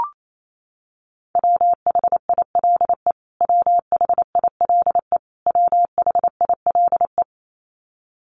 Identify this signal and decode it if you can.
Morse code, heard as a keyed tone, about 28 wpm, 700 Hz. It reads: WHILE WHILE WHILE
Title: Top 500 Words Repeated 28wpm